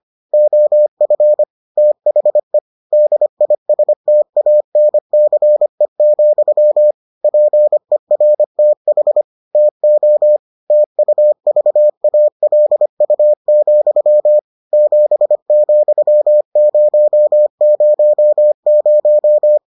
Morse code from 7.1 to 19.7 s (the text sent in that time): PERTH TO TUVALU, 7,000